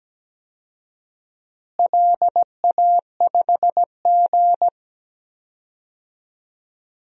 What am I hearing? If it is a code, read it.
Morse code, heard as a keyed tone, about 17 wpm, 700 Hz. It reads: LA5G
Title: DXCC-100 Call Signs - 4 Characters - Repeated 3x - 17wpm